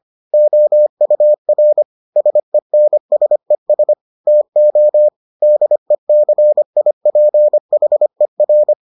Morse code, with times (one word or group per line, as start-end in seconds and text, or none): 0.3-1.8 OUR
2.2-3.9 SENSES
4.3-5.1 TO
5.4-8.7 DECIPHER